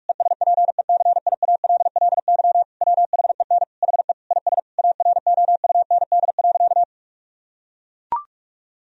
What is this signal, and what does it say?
Morse code, heard as a keyed tone, about 41 wpm, 700 Hz. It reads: ESPECIALLY WHEN HE IS AROUND.
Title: Sentences from Top 1000 Words - Review Part 4 - 45wpm